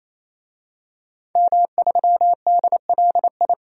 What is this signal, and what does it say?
Morse code, heard as a keyed tone, about 27 wpm, 700 Hz. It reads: M3DLI
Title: DXCC-100 Call Signs - 5 Characters - 28wpm